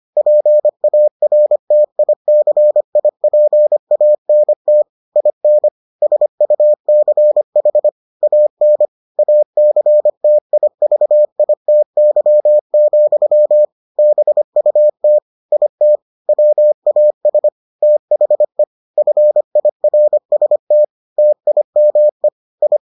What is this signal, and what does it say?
Morse code, heard as a keyed tone, about 25 wpm, 600 Hz. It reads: PARTICIPANT IN SUCH AN ACTIVITY, BUT IT WAS THE FIRST TIME I